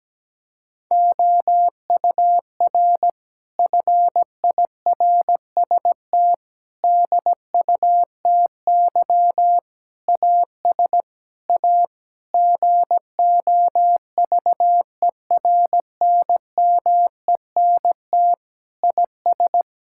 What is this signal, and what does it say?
Morse code, heard as a keyed tone, about 17 wpm, 700 Hz. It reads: OUR FIRST DUTY AS A GOVERNMENT IS
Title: Sentences from Top 1600 Words 17wpm